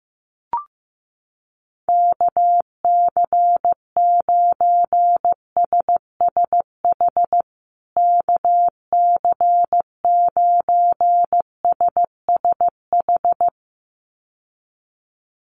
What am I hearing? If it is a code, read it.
Morse code, heard as a keyed tone, about 15 wpm, 700 Hz. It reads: KC9SSH KC9SSH
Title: USA Call Signs Repeated 2x 15wpm